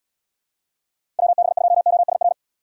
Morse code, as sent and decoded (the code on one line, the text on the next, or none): .-- -... ..--- --- ... .-.
WB2OSR